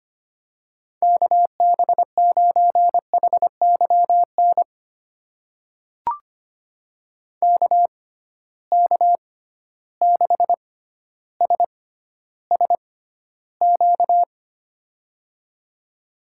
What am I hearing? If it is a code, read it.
Morse code, heard as a keyed tone, about 25 wpm, spaced as for 13 wpm, 700 Hz. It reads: KB9HYN KK6SSQ
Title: USA Call Signs - 6 Characters - 6x Character Spacing - 25wpm